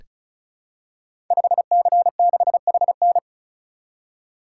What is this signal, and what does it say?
Morse code, heard as a keyed tone, about 36 wpm, 700 Hz. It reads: 5C6HN
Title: Code Groups - 5 Letter:Numbers - 35wpm